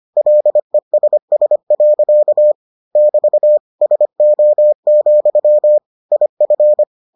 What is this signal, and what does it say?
Morse code, heard as a keyed tone, about 25 wpm, 600 Hz. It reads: LESS. <BT> SO, IF